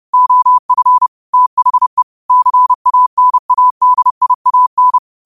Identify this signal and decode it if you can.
Morse code, heard as a keyed tone, about 30 wpm, 1000 Hz. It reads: OF THE CANADIAN